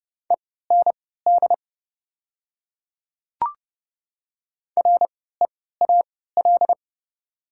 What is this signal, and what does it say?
Morse code, heard as a keyed tone, about 30 wpm, spaced as for 17 wpm, 700 Hz. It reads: END REAL
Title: 3x Character Spacing - Top 300 Words - 30wpm